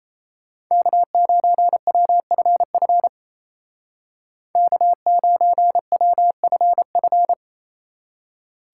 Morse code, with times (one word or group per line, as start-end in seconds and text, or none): 0.7-3.1 K9WFF
4.6-7.3 K9WFF